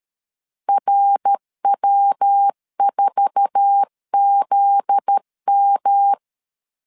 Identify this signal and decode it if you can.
Morse code, heard as a keyed tone, about 13 wpm, 800 Hz. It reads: RW4ZM
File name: LEZIONI / 8 YJ45 / 8 YJ4 12wpm 2151